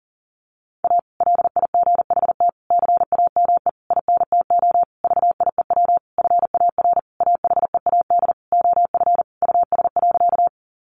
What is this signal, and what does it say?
Morse code, heard as a keyed tone, about 37 wpm, 700 Hz. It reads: A LIGHT CAME INTO VIEW FAR AHEAD OF US.
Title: Sentences from Top 1100 Words 40wpm